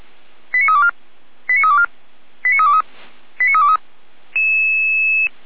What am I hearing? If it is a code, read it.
Five-tone selective call (ZVEI), heard as quick runs of five tones, tones of about 70 ms. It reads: 89324 89324 89323 89323
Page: SDR/ZVEI/89338